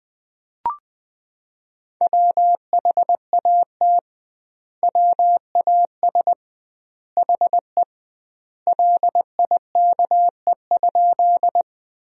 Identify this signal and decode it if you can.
Morse code, heard as a keyed tone, about 20 wpm, 700 Hz. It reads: WHAT WAS HE LIKE?
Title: 2x Word-Spacing - Sentences from Top 100 Words - 20wpm